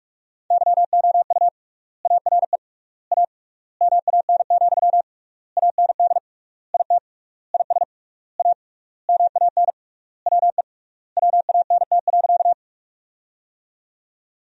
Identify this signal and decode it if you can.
Morse code, heard as a keyed tone, about 46 wpm, 700 Hz. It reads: YOU ARE A MAN, AND IT IS A MAN WE WANT.
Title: 3x Word-Spacing - Sentences from Top 200 Words - 45wpm